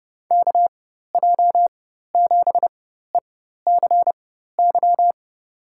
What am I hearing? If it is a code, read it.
Morse code, heard as a keyed tone, about 31 wpm, spaced as for 14 wpm, 700 Hz. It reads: KJ7ECY
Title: USA Call Signs - 6 Characters - 4x Character Spacing - 30wpm